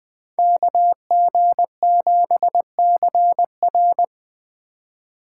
Morse code, heard as a keyed tone, about 21 wpm, 700 Hz. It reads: KG7CR